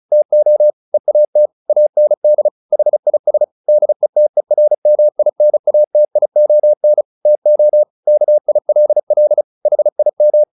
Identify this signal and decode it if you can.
Morse code, heard as a keyed tone, about 36 wpm, 600 Hz. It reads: TO EAT AND HIS DETERMINATION TO KILL HIM
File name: the old man and the sea 2 35wpm